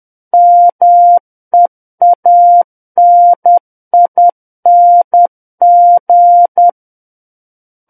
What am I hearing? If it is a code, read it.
Morse code, heard as a keyed tone, about 10 wpm, 700 Hz. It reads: MEANING
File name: Book One / 01-Chapter 1 - 10 WPM